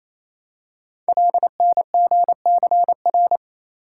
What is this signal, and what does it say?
Morse code, heard as a keyed tone, about 27 wpm, 700 Hz. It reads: LNGCR
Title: Code Groups - 5 Letters - 28wpm